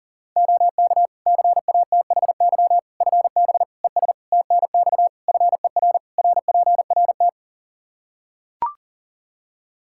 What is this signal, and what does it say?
Morse code, heard as a keyed tone, about 37 wpm, 700 Hz. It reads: OK CATHY FB ES TNX FER RPRT
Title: K1USN SST QSO Practice 40wpm